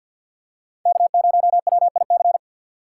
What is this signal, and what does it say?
Morse code, heard as a keyed tone, about 47 wpm, 700 Hz. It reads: K0WIC
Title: USA Call Signs - 5 Characters - 6x Character Spacing - 50wpm